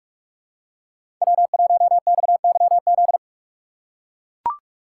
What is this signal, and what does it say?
Morse code, heard as a keyed tone, about 44 wpm, 700 Hz. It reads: W1XYZ
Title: USA Call Signs - 5 Characters - 6x Character Spacing - 45wpm